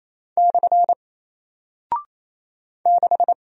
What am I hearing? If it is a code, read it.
Morse code, heard as a keyed tone, about 28 wpm, 700 Hz. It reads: / 6
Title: Single Character (Letter:Number:Punctuation:Prosign) Alt 28wpm